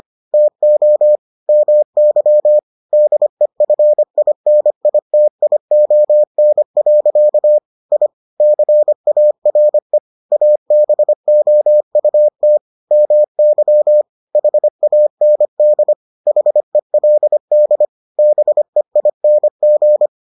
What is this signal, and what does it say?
Morse code, heard as a keyed tone, about 25 wpm, 600 Hz. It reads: TO MY DEFINITION. I CARE ABOUT MY HAND HELD BEING